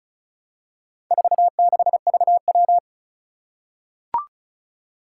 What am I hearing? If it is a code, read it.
Morse code, heard as a keyed tone, about 36 wpm, 700 Hz. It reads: 46VW
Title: Four Challenging Letter-Number Combinations 35wpm